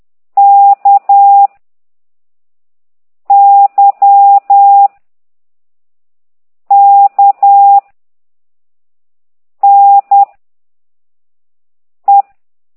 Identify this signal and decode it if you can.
Morse code, heard as a keyed tone, about 10 wpm, 800 Hz. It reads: K Y K N E